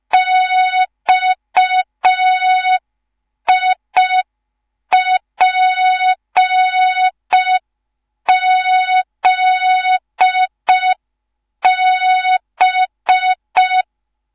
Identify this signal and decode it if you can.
Morse code, heard as a keyed tone, about 5 wpm, 3700 Hz. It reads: XIPZB
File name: Morze-HA2MN / hang / 10GWZ HA2MN